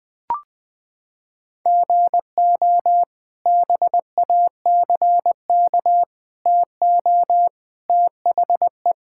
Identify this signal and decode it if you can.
Morse code, heard as a keyed tone, about 20 wpm, 700 Hz. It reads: GO BACK TO THE
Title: Sentences with 3-Word Phrases - 20wpm